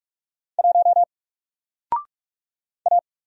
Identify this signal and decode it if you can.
Morse code, heard as a keyed tone, about 44 wpm, 700 Hz. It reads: 1 A
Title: Single Character (Letter:Number:Punctuation:Prosign) 45wpm